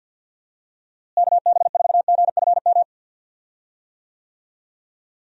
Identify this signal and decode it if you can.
Morse code, heard as a keyed tone, about 52 wpm, 700 Hz. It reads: KB4GFM
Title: USA Call Signs - 6 Characters - Repeated 2x - 50wpm